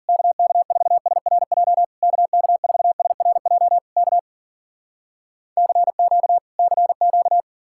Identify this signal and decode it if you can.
Morse code, heard as a keyed tone, about 53 wpm, 700 Hz. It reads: KK4SRJ KK4SRJ K CQ CQ